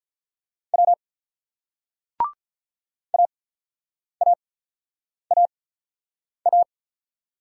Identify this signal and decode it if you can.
Morse code, heard as a keyed tone, about 43 wpm, spaced as for 8 wpm, 700 Hz. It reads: W AAAA
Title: Speed-Racing Single Letters 35wpm